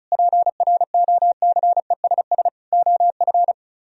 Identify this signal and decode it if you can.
Morse code, heard as a keyed tone, about 42 wpm, 700 Hz. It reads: PROCESS OF
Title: Speed-Racing - Sets of 4 Words 35wpm